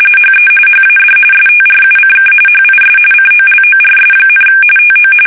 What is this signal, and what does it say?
RTTY, about 45.45 baud. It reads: HE HAD STRUCK HIS COLORS, TO
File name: fsk850 / usnavy1